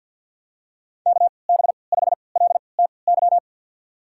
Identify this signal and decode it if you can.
Morse code, heard as a keyed tone, about 53 wpm, spaced as for 29 wpm, 700 Hz. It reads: KB5LTY
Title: USA Call Signs - 6 Characters - 3x Character Spacing - 50wpm